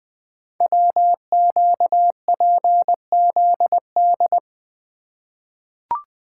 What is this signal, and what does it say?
Morse code, heard as a keyed tone, about 20 wpm, 700 Hz. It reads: WQPZD
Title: Code Groups - 5 Letter:Numbers - 20wpm